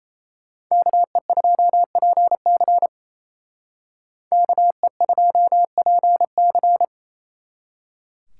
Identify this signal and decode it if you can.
Morse code, heard as a keyed tone, about 36 wpm, 700 Hz. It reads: KE2PC KE2PC